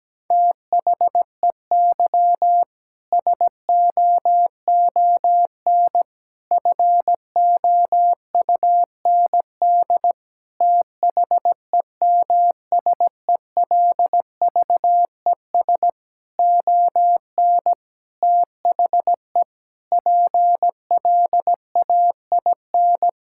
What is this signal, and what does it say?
Morse code, heard as a keyed tone, about 17 wpm, 700 Hz. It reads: THEY SOON FOUND THEMSELVES ON THE PLAIN
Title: Sentences from Top 1000 Words - Review Part 5 - 17wpm